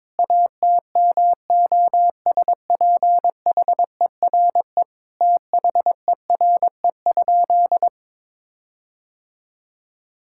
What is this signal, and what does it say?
Morse code, heard as a keyed tone, about 22 wpm, 700 Hz. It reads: ATMOSPHERE THERE?
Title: Sentences from Top 1600 Words 22wpm